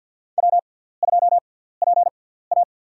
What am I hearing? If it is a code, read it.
Morse code, heard as a keyed tone, about 51 wpm, spaced as for 17 wpm, 700 Hz. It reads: W2PA
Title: USA Call Signs - 4 Characters - 6x Character Spacing - 50wpm